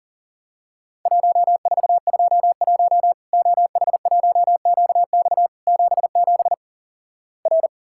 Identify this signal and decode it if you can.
Morse code, heard as a keyed tone, about 43 wpm, 700 Hz. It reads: 1421 OH1QX 77 R
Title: Worked All Europe DX Contest - QTC Exchange - 40wpm